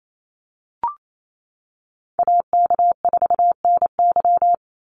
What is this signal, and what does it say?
Morse code, heard as a keyed tone, about 28 wpm, 700 Hz. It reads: AK4NY
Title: Call Signs - 28wpm